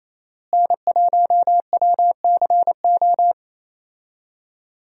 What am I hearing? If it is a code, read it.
Morse code, heard as a keyed tone, about 28 wpm, 700 Hz. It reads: N1WCO
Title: USA Call Signs - 5 Characters - Repeated 3x - 28wpm